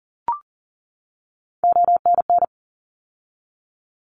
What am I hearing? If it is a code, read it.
Morse code, heard as a keyed tone, about 40 wpm, 700 Hz. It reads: ONN
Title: ARRL Sections 40wpm